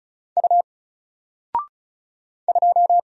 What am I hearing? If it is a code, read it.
Morse code, heard as a keyed tone, about 36 wpm, 700 Hz. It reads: U 2
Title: Single Letter-Number Alt 35wpm